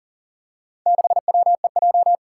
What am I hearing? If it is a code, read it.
Morse code, heard as a keyed tone, about 41 wpm, 700 Hz. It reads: 6WEJ
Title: Four Challenging Letter-Number Combinations 40wpm